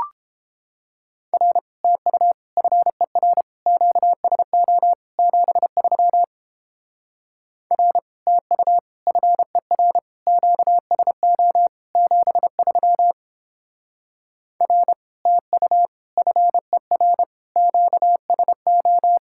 Morse code, heard as a keyed tone, about 29 wpm, 700 Hz. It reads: R TU FER QSO 73 R TU FER QSO 73 R TU FER QSO